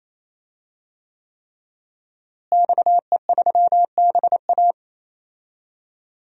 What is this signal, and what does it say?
Morse code, heard as a keyed tone, about 28 wpm, 700 Hz. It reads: XE3BA
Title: DXCC-100 Call Signs - 5 Characters - Repeated 3x - 28wpm